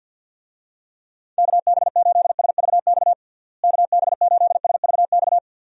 Morse code, heard as a keyed tone, about 52 wpm, 700 Hz. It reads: KB8SVX KB8SVX
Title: USA Call Signs - 6 Characters - Repeated 2x - 50wpm